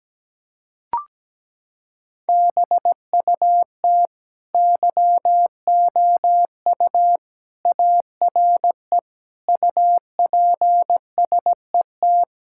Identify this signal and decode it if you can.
Morse code, heard as a keyed tone, about 17 wpm, 700 Hz. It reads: BUT YOU ARE UPSET
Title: Sentences from Top 1000 Words - Review Part 5 - 17wpm